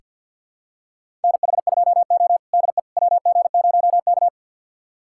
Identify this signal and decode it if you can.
Morse code, heard as a keyed tone, about 47 wpm, 700 Hz. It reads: NH2O DE WG0K
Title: USA Callsign DE Callsign - 4 Characters - 50wpm